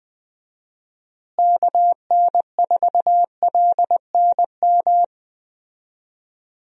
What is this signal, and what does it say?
Morse code, heard as a keyed tone, about 20 wpm, 700 Hz. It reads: KN4LNM
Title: USA Call Signs - 6 Characters - Repeated 2x - 20wpm